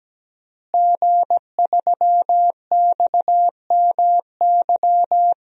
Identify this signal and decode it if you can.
Morse code, heard as a keyed tone, about 17 wpm, 700 Hz. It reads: G3XMY
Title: DXCC-100 Call Signs - 5 Characters - Repeated 2x - 17wpm